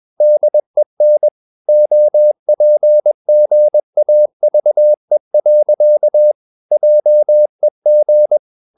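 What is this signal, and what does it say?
Morse code, heard as a keyed tone, about 21 wpm, 600 Hz. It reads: DEN OPGAVE. JEG
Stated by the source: Morse practice at 21 wpm, slow-speed holiday exercise.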